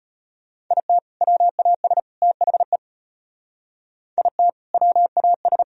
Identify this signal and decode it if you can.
Morse code, heard as a keyed tone, about 34 wpm, 700 Hz. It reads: IT WAS THE IT WAS